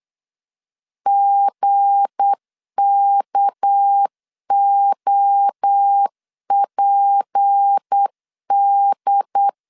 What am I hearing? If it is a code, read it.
Morse code, heard as a keyed tone, about 8 wpm, 800 Hz. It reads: GKOPD
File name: LEZIONI / 7 GF9Z / 7 GF 8wpm 3108